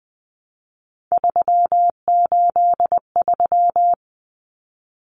Morse code, heard as a keyed tone, about 20 wpm, 700 Hz. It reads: 383
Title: Numbers - 3-Digit 20wpm